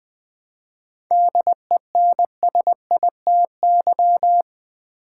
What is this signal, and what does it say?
Morse code, heard as a keyed tone, about 20 wpm, 690 Hz. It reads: DENSITY
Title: 7 Letter Words - First 3 Letters Given - 20wpm